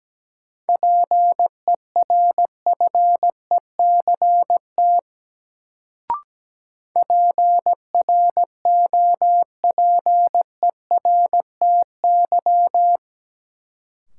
Morse code, heard as a keyed tone, about 17 wpm, 700 Hz. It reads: PERFECT PROPERTY
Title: Sentences from Top 1000 Words - New Word Drill - 17wpm